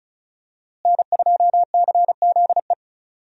Morse code, heard as a keyed tone, about 36 wpm, 700 Hz. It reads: N2CZE
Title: USA Call Signs - 5 Characters - Repeated 2x - 35wpm